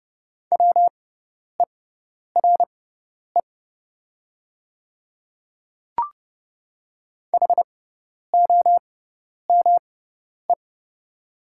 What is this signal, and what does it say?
Morse code, heard as a keyed tone, about 30 wpm, spaced as for 10 wpm, 700 Hz. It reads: WERE HOME
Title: 6x Character Spacing - Top 300 Words - 30wpm